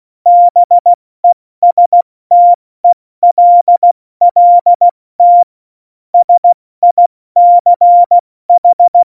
Morse code, heard as a keyed tone, about 16 wpm, 700 Hz. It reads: BESTELLT SICH
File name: Morsekurs / Witze / Witz 22. Dezember 2020, 70 F80